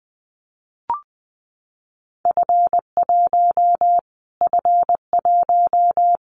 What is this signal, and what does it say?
Morse code, heard as a keyed tone, about 20 wpm, 700 Hz. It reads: F1 F1
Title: Callsign Prefixes for Top 25 DXCC - Repeated 3x - 20wpm